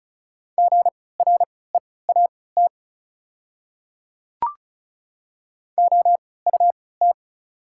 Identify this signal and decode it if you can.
Morse code, heard as a keyed tone, about 34 wpm, spaced as for 20 wpm, 700 Hz. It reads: GREAT OUT
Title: 3x Character Spacing - Top 200 Words - 35wpm